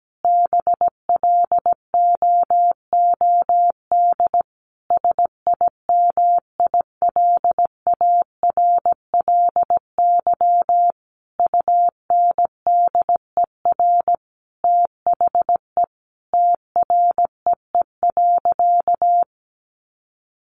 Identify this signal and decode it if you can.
Morse code, heard as a keyed tone, about 17 wpm, 700 Hz. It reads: BLOOD SIMILARLY UNDER THE TREE.
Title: Sentences from Top 1400 Words 17wpm